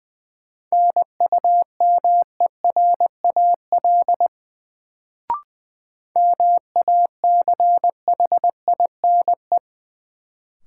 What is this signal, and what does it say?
Morse code, heard as a keyed tone, about 21 wpm, 700 Hz. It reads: NUMERAL MACHINE
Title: Sentences from Top 1000 Words - Review Part 2 - New Word Drill - 20wpm